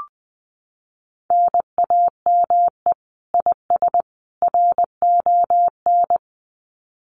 Morse code, heard as a keyed tone, about 20 wpm, 700 Hz. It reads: NAME IS RON
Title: Common QSO Phrases 20wpm